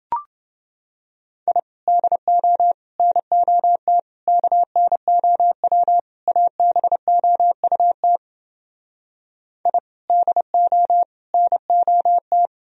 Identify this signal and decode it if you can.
Morse code, heard as a keyed tone, about 28 wpm, 700 Hz. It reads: I DO NOT KNOW ABOUT I DO NOT